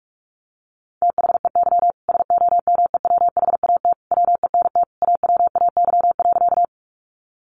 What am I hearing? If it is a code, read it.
Morse code, heard as a keyed tone, about 44 wpm, 700 Hz. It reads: THEY SOMEWHAT WENT AWAY.
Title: Sentences from Top 1300 Words 45wpm